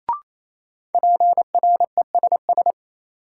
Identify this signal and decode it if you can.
Morse code, heard as a keyed tone, about 28 wpm, 700 Hz. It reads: PRESS
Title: Sentences from Top 800 Words - New Word Drill - 28wpm